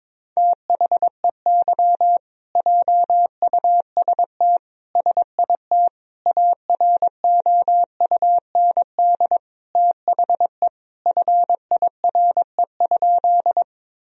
Morse code, heard as a keyed tone, about 22 wpm, 700 Hz. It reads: THEY JUST SIT AROUND THE FIRE?
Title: Sentences from Top 1000 Words - Review Part 3 - 22wpm